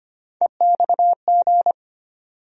Morse code, heard as a keyed tone, about 25 wpm, 700 Hz. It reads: EXG
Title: USA Call Sign Postfixes - 3 Letters 25wpm